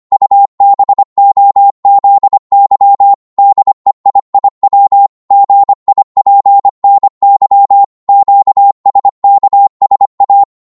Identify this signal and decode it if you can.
Morse code, heard as a keyed tone, about 25 wpm, 800 Hz. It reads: UBOZY DEIIW GIPNY QSKSA
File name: Morse / ltr-25wpm